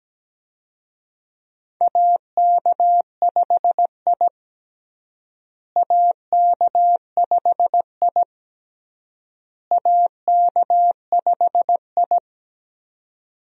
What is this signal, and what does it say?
Morse code, heard as a keyed tone, about 17 wpm, 700 Hz. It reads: AK5I AK5I AK5I
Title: USA Call Signs - 4 Characters - Repeated 3x - 17wpm